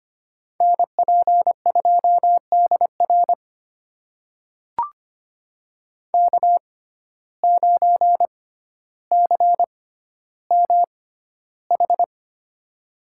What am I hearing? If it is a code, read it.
Morse code, heard as a keyed tone, about 25 wpm, spaced as for 13 wpm, 700 Hz. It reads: NP2DR K9CMH
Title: USA Call Signs - 5 Characters - 6x Character Spacing - 25wpm